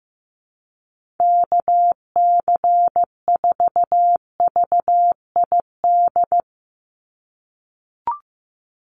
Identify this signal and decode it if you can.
Morse code, heard as a keyed tone, about 15 wpm, 700 Hz. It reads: KC4VID
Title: USA Call Signs 15wpm